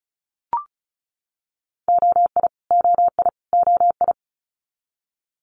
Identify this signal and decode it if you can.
Morse code, heard as a keyed tone, about 36 wpm, 700 Hz. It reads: OI OI OI
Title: Multi-letter Phonemes Repeated 3x 35wpm